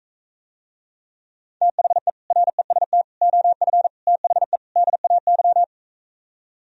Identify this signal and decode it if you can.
Morse code, heard as a keyed tone, about 45 wpm, 700 Hz. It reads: THE REST OF THE DAY